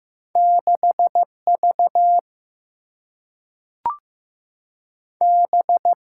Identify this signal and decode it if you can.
Morse code, heard as a keyed tone, about 15 wpm, 700 Hz. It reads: 6 V B
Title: Single-Letter Single-Number 15wpm